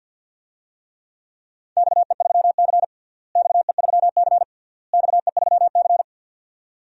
Morse code, heard as a keyed tone, about 50 wpm, 700 Hz. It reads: XE3C XE3C XE3C
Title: DXCC-100 Call Signs - 4 Characters - Repeated 3x - 50wpm